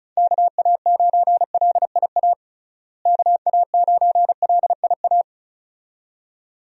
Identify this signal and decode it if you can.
Morse code, heard as a keyed tone, about 35 wpm, 700 Hz. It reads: KA9LIA KA9LIA
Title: USA Call Signs - 6 Characters - Repeated 2x - 35wpm